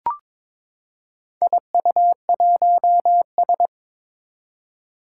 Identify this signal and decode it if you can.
Morse code, heard as a keyed tone, about 22 wpm, 700 Hz. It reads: IU1S
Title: Four Challenging Letter-Number Combinations 22wpm